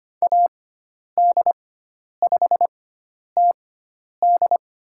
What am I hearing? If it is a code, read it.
Morse code, heard as a keyed tone, about 25 wpm, spaced as for 10 wpm, 700 Hz. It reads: AD5TD
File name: Lesson 16g - Letter c - Call Signs 25wpm